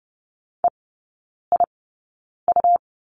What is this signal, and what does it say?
Morse code, heard as a keyed tone, about 30 wpm, spaced as for 9 wpm, 690 Hz. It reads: EIU
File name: Lesson 17e - Letter u - Three Characters 30wpm